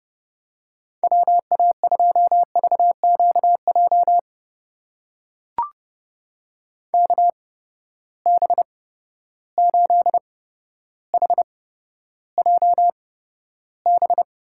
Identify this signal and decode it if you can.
Morse code, heard as a keyed tone, about 30 wpm, spaced as for 13 wpm, 700 Hz. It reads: WA2VQJ KB8HJB
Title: USA Call Signs - 6 Characters - 8x Character Spacing - 30wpm